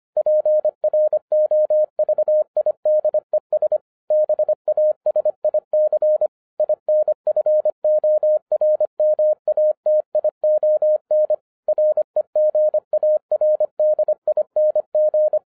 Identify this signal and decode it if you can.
Morse code, heard as a keyed tone, about 25 wpm, 610 Hz. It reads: PROVIDES BASIC INFORMATION REGARDING